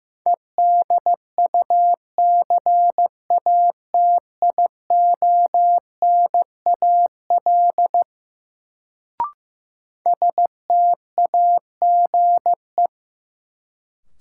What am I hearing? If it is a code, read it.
Morse code, heard as a keyed tone, about 15 wpm, 700 Hz. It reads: EDUCATIONAL STAGE
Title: Sentences from Top 1200 Words - New Word Drill - 15wpm